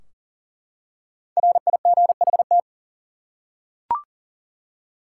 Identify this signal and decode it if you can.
Morse code, heard as a keyed tone, about 40 wpm, 700 Hz. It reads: RIGHT
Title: Five Letter Words - 40wpm